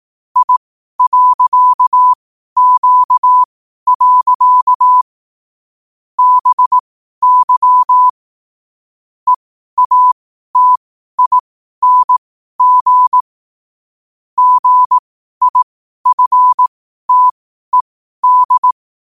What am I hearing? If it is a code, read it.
Morse code, heard as a keyed tone, about 18 wpm, spaced as for 12 wpm, 1000 Hz. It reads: I.Q. BY EATING GIFTED
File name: qotdfiles / 12wpm